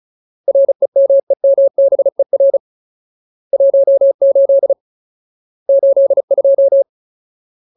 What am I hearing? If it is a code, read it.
Morse code, heard as a keyed tone, about 35 wpm, 550 Hz. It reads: REMEMBER 18 82
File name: Ultimate Word List / 35WPM Ultimate